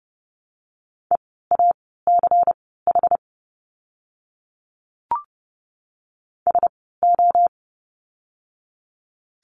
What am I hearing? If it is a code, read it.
Morse code, heard as a keyed tone, about 30 wpm, spaced as for 17 wpm, 700 Hz. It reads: EACH SO
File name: course / 30:10wpm / 230 Lesson 37f - Letter x - Words 30wpm